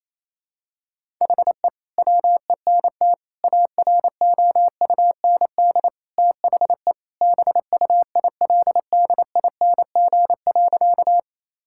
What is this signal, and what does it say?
Morse code, heard as a keyed tone, about 28 wpm, 700 Hz. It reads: HE WENT AROUND THE BUILDING.
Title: Sentences from Top 1000 Words - Review Part 5 - 28wpm